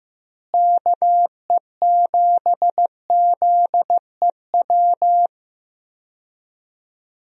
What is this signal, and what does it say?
Morse code, heard as a keyed tone, about 15 wpm, 700 Hz. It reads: KE7ZEW
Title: USA Call Signs - 6 Characters - Repeated 3x - 15wpm